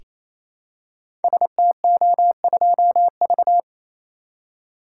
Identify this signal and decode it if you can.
Morse code, heard as a keyed tone, about 27 wpm, 700 Hz. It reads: STO2V
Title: Code Groups - 5 Letter:Numbers - 28wpm